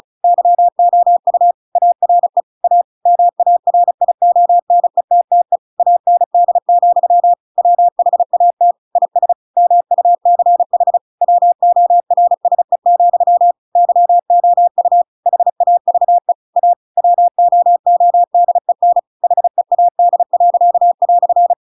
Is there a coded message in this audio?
Yes: Morse code from 0.2 to 21.5 s, heard as a keyed tone, about 34 wpm, 700 Hz. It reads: YOU ARE A MARIONETTE AND, WHAT IS MUCH WORSE, YOU HAVE A WOODEN HEAD."